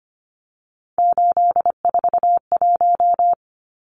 Morse code, heard as a keyed tone, about 25 wpm, 690 Hz. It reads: 841
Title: Numbers - 3-Digit 25wpm